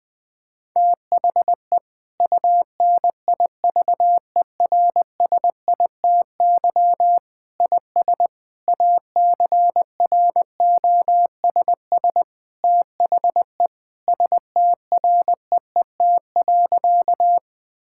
Morse code, heard as a keyed tone, about 20 wpm, 700 Hz. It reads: THE UNIVERSITY IS ACROSS THE STREET.
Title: Sentences from Top 1000 Words - Review Part 5 - 20wpm